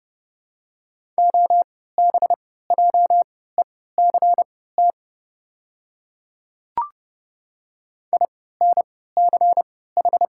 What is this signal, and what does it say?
Morse code, heard as a keyed tone, about 30 wpm, spaced as for 17 wpm, 690 Hz. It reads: OBJECT INCH
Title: 3x Character Spacing - Top 500 Words - 30wpm